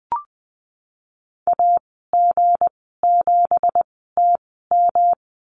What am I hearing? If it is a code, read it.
Morse code, heard as a keyed tone, about 20 wpm, spaced as for 15 wpm, 700 Hz. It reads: AG7TM
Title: 173 Lesson 28g - Number 9 - Call Signs 20wpm